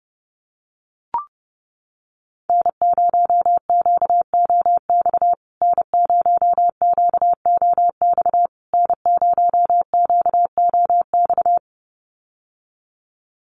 Morse code, heard as a keyed tone, about 30 wpm, 700 Hz. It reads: N0QOX N0QOX N0QOX
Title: USA Call Signs Repeated 3x 30wpm